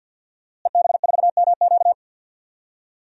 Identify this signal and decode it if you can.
Morse code, heard as a keyed tone, about 52 wpm, 700 Hz. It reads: EB4MQ
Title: DXCC-100 Call Signs - 5 Characters - Repeated 3x - 50wpm